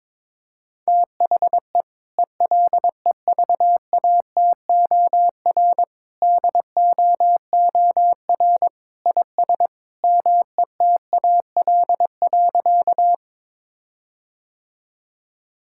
Morse code, heard as a keyed tone, about 22 wpm, 700 Hz. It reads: THE ELEVATOR DOOR IS METAL.
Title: Sentences from Top 1600 Words 22wpm